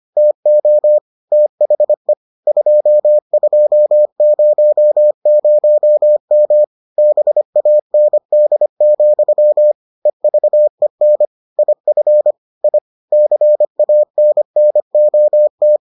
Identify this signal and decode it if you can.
Morse code, heard as a keyed tone, about 25 wpm, 600 Hz. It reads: TO THE 2200M BAND, EVEN IF I CANNOT